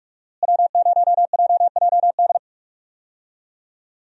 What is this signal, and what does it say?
Morse code, heard as a keyed tone, about 46 wpm, 700 Hz. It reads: W0JJD
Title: USA Call Signs - 5 Characters - Repeated 3x - 45wpm